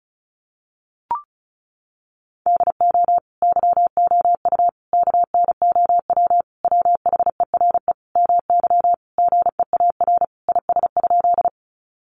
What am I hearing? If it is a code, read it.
Morse code, heard as a keyed tone, about 34 wpm, 700 Hz. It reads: DO YOU KNOW WHERE MY GEAR IS?
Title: Sentences from Top 1400 Words 35wpm